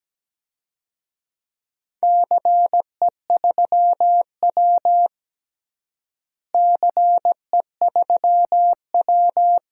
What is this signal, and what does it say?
Morse code, heard as a keyed tone, about 17 wpm, 700 Hz. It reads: CE3W CE3W
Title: DXCC-100 Call Signs - 4 Characters - Repeated 3x - 17wpm